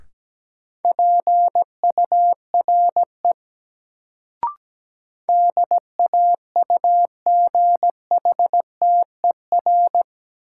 Morse code, heard as a keyed tone, about 17 wpm, 700 Hz. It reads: PURE DAUGHTER
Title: Sentences from Top 1500 Words - New Word Drill - 17wpm